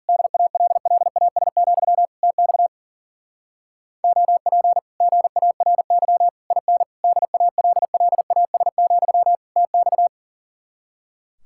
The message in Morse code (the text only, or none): DALLAS, TX OP GARY IN DALLAS, TX